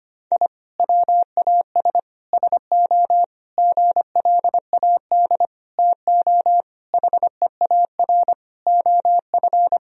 Morse code, heard as a keyed tone, about 25 wpm, 700 Hz. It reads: I WAS SO GLAD TO HEAR OF
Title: Sentences from Top 1000 Words - Review Part 4 - 25wpm